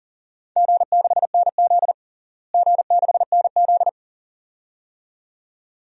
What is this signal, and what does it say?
Morse code, heard as a keyed tone, about 40 wpm, 700 Hz. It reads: G6NZ G6NZ
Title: DXCC-100 Call Signs - 4 Characters - Repeated 2x - 40wpm